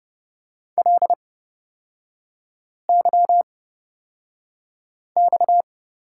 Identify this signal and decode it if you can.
Morse code, heard as a keyed tone, about 30 wpm, spaced as for 5 wpm, 700 Hz. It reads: LYX
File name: Lesson 37e - Letter x - Three Characters 30wpm